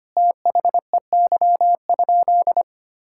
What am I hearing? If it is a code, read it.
Morse code, heard as a keyed tone, about 25 wpm, 700 Hz. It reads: THEY?
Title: Sentences from Top 1000 Words - Review Part 5 - 25wpm